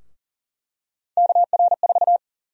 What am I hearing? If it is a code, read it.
Morse code, heard as a keyed tone, about 40 wpm, 700 Hz. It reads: KR4
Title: USA Call Sign Prefixes - Repeated 2x - 40wpm